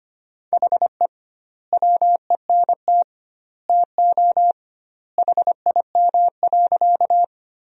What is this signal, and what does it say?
Morse code, heard as a keyed tone, about 25 wpm, 700 Hz. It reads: HE WENT TO HIM.
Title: 2x Word-Spacing - Sentences from Top 200 Words - 25wpm